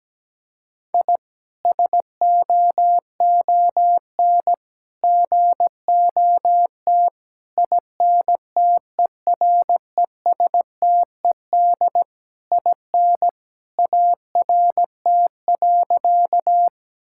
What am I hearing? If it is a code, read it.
Morse code, heard as a keyed tone, about 17 wpm, 700 Hz. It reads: I SOON GOT INTERESTED IN ART.
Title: Sentences from Top 1000 Words - Review Part 6 - 17wpm